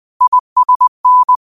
Morse code, heard as a keyed tone, about 20 wpm, 1000 Hz. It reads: ISN
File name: qotdfiles / 20wpm